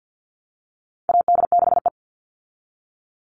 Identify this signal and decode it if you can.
Morse code, heard as a keyed tone, about 50 wpm, 700 Hz. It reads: AD6E
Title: Call Signs - 50wpm